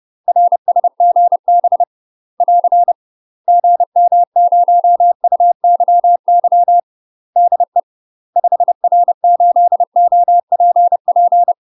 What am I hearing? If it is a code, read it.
Morse code, heard as a keyed tone, about 30 wpm, 700 Hz. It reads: RSGB <AR> GM0UYY DE 5R8OPP
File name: UK MASTER QSOs TRAINING / 30WPM / UK MASTER QSOs 0008